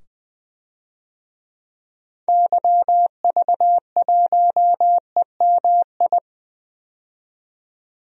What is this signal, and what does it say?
Morse code, heard as a keyed tone, about 20 wpm, 700 Hz. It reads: YV1EMI
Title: DXCC-100 Call Signs - 6 Characters - 20wpm